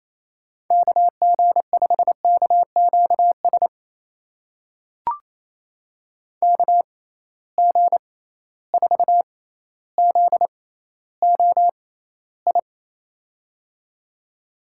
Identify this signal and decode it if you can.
Morse code, heard as a keyed tone, about 28 wpm, spaced as for 14 wpm, 700 Hz. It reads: KG5KQS KG4ZOI
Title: USA Call Signs - 6 Characters - 6x Character Spacing - 28wpm